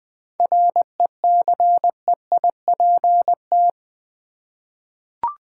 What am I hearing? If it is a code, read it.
Morse code, heard as a keyed tone, about 20 wpm, 700 Hz. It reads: RECEIPT
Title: 7 Letter Words - First 3 Letters Given - 20wpm